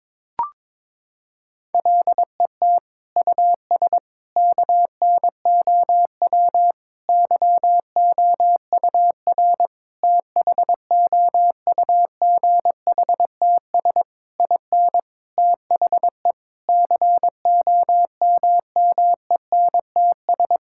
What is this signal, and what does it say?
Morse code, heard as a keyed tone, about 22 wpm, 700 Hz. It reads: LET US KNOW YOUR THOUGHTS IN THE COMMENTS
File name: Sets of 9 Words - 22wpm